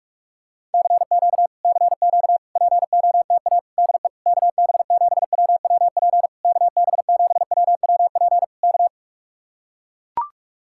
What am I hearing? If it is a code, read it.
Morse code, heard as a keyed tone, about 44 wpm, 700 Hz. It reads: CQ CQ POTA DE KB7WWP KB7WWP K